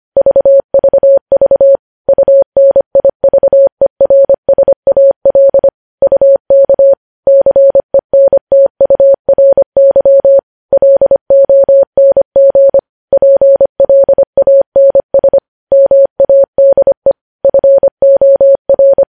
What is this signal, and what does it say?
Morse code, heard as a keyed tone, about 25 wpm, 550 Hz. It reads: VVV UNIVERSAL UK CENTURY LONG PLANS MADE FOR
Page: News Headlines in Morse Code at 25 WPM